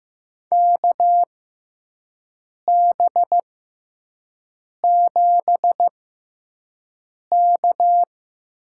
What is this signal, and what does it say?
Morse code, heard as a keyed tone, about 15 wpm, spaced as for 5 wpm, 700 Hz. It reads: KB7K
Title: USA Call Signs - 6 Characters - 6x Character Spacing - 15wpm